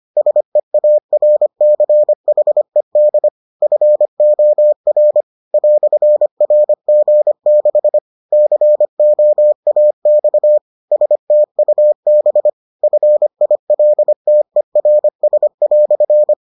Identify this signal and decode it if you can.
Morse code, heard as a keyed tone, about 25 wpm, 600 Hz. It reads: SEARCHED FOR "RG6 COAX STUB FILTERS"